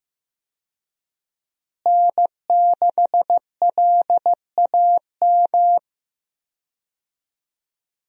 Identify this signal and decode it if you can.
Morse code, heard as a keyed tone, about 15 wpm, 700 Hz. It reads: N6LAM
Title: USA Call Signs - 5 Characters - Repeated 3x - 15wpm